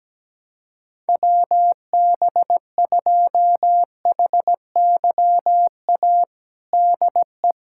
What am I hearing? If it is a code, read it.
Morse code, heard as a keyed tone, about 17 wpm, 700 Hz. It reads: WB2HYA DE
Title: USA Callsign DE Callsign - 6 Characters - 17wpm